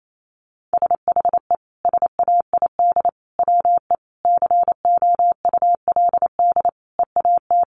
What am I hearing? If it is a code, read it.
Morse code, heard as a keyed tone, about 28 wpm, 700 Hz. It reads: SHE SAID WE COULD EAT
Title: Sentences from Top 1300 Words 28wpm